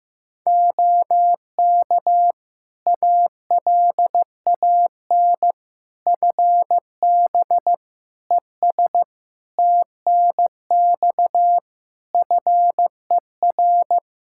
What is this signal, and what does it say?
Morse code, heard as a keyed tone, about 15 wpm, 700 Hz. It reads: OK ALAN FB ES TNX FER
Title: Practice QSO - Exchange 2 - Info - 15wpm